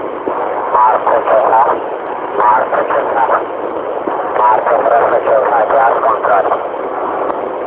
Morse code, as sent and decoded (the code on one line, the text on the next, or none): none